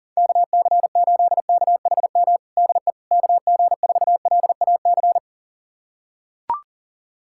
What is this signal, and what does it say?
Morse code, heard as a keyed tone, about 37 wpm, 700 Hz. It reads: KC8KHM DE KG4LAC
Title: USA Callsign DE Callsign - 6 Characters - 40wpm